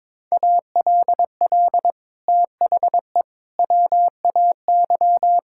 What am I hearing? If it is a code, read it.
Morse code, heard as a keyed tone, about 22 wpm, 700 Hz. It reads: ALL THE WAY
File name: practice / 22wpm / Sets of 6 Words - 22wpm